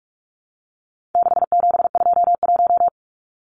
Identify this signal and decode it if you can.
Morse code, heard as a keyed tone, about 44 wpm, 700 Hz. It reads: 6721
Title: Numbers - 4-Digit 45wpm